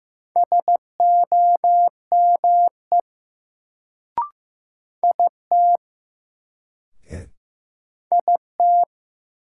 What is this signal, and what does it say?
Morse code, heard as a keyed tone, about 15 wpm, 700 Hz. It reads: SOME IT IT
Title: Sentences from Top 100 Words - New Word Drill - 15wpm